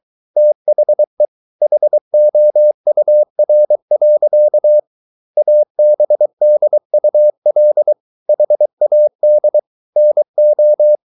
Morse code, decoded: THE HOUR. ABDUL HAD NO

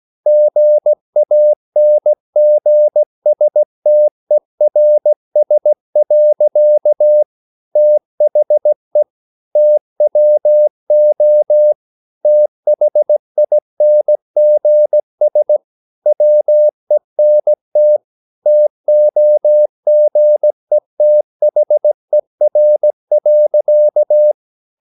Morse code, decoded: GANGSTERS. THE TWO THINGS WENT TOGETHER.

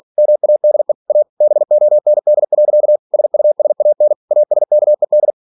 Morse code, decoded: MADE A BOND. SUSAN ASKED

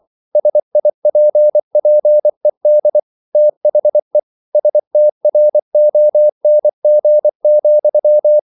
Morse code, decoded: SIPPED THE STRONG,

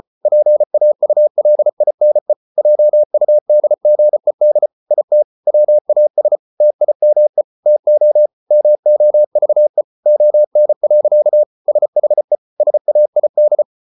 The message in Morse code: PAULINE JUDGED IT WAS TIME TO MOVE ON. SHE SAID

